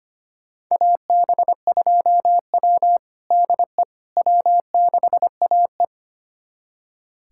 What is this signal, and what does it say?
Morse code, heard as a keyed tone, about 25 wpm, 700 Hz. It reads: AB2W DE W6AE
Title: USA Callsign DE Callsign - 4 Characters - 25wpm